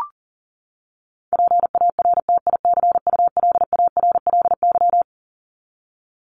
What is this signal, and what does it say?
Morse code, heard as a keyed tone, about 40 wpm, 700 Hz. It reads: PARTICULARLY
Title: LY Suffix - Top 100 Words with Base Words 40wpm